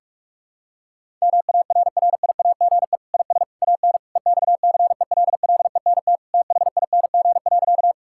Morse code, heard as a keyed tone, about 48 wpm, 700 Hz. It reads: MARRIAGE IS AN EXCELLENT THING.
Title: Sentences from Top 1000 Words - Review Part 5 - 45wpm